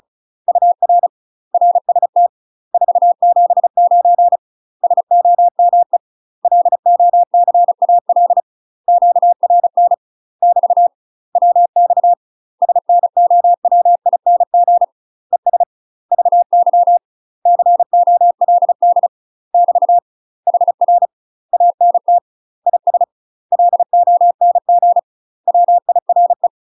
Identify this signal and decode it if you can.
Morse code, heard as a keyed tone, about 35 wpm, 700 Hz. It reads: UR RST 479 SOME LOCAL QRN <BT> WX SNOWING ES VY COLD <BT> HR ANT IS LONG WIRE
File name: UK MASTER QSOs TRAINING / 35WPM / UK MASTER QSOs 0001